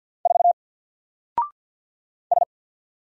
Morse code, heard as a keyed tone, about 50 wpm, 700 Hz. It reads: 4 S
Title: Single Letter-Number 50wpm